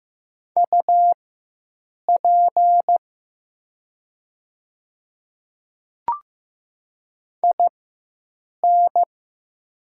Morse code, heard as a keyed tone, about 15 wpm, spaced as for 7 wpm, 700 Hz. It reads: UP IN
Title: 4x Character Spacing - Top 100 Words - 15wpm